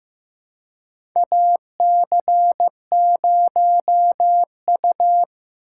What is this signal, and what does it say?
Morse code, heard as a keyed tone, about 15 wpm, 700 Hz. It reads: AC0U
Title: USA Call Signs - 4 Characters - 2x Character Spacing - 15wpm